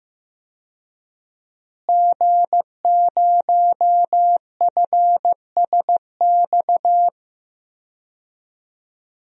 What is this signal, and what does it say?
Morse code, heard as a keyed tone, about 15 wpm, 700 Hz. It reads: G0FSX
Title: DXCC-100 Call Signs - 5 Characters - Repeated 3x - 15wpm